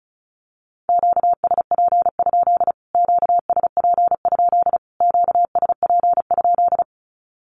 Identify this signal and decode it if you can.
Morse code, heard as a keyed tone, about 34 wpm, 700 Hz. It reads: QSP? QSP? QSP?
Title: Q-Codes Repeated 3x 35wpm